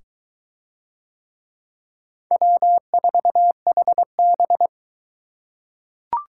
Morse code, heard as a keyed tone, about 22 wpm, 700 Hz. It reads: W4HB